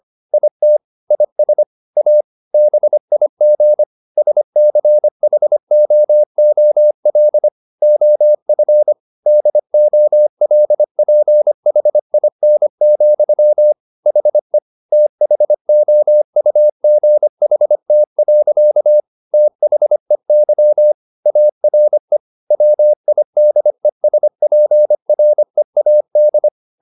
Morse code, heard as a keyed tone, about 25 wpm, 600 Hz. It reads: IT IS A BIG SCHOOL OF DOLPHIN, HE THOUGHT. THEY ARE WIDESPREAD
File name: the old man and the sea part 1 25wpm